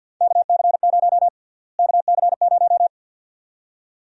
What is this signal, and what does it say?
Morse code, heard as a keyed tone, about 52 wpm, 700 Hz. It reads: KC0 KC0
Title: USA Call Sign Prefixes - Repeated 2x - 50wpm